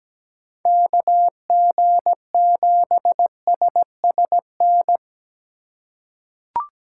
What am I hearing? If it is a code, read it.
Morse code, heard as a keyed tone, about 17 wpm, 700 Hz. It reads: KG7SSN
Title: USA Call Signs - 6 Characters - 2x Character Spacing - 17wpm